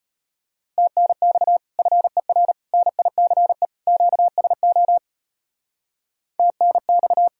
Morse code, heard as a keyed tone, about 32 wpm, 700 Hz. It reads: TNX FER NICE QSO TNX